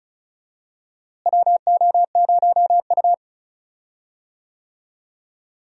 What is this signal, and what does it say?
Morse code, heard as a keyed tone, about 34 wpm, 700 Hz. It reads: WO0U
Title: USA Call Signs - 4 Characters - Repeated 3x - 35wpm